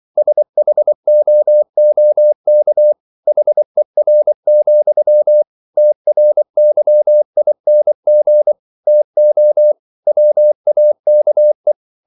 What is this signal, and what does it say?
Morse code, heard as a keyed tone, about 24 wpm, 600 Hz. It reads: SHOOK HER, TRYING TO WAKE